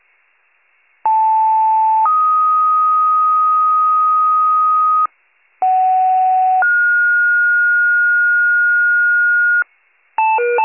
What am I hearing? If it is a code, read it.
Two-tone page, heard as a one-second tone, then a three-second tone, about 880 then 1250 Hz, about 750 then 1430 Hz, with noise. It is heard from 1.0 to 9.6 s.